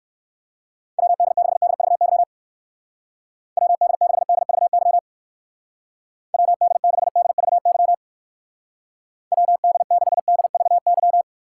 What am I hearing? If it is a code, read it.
Morse code, heard as a keyed tone, about 59 wpm, 700 Hz. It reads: WD6DVY WD6DVY WD6DVY WD6DVY